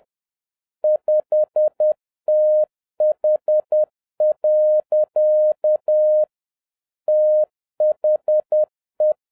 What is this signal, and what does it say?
Morse code, heard as a keyed tone, about 10 wpm, 610 Hz. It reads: 5TH. THE